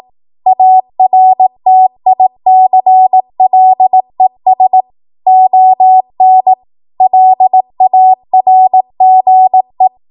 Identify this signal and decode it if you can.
Morse code, heard as a keyed tone, about 18 wpm, 750 Hz. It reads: ARTICLES ON LARGE
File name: Archive / 18 WPM / 18WPM